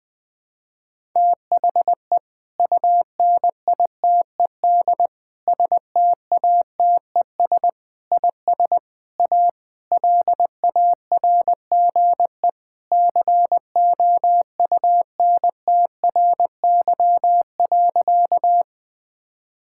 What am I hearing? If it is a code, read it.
Morse code, heard as a keyed tone, about 20 wpm, 700 Hz. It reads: THE UNITED STATES IS A LARGE COUNTRY.
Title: Sentences with 4-Word Phrases - 20wpm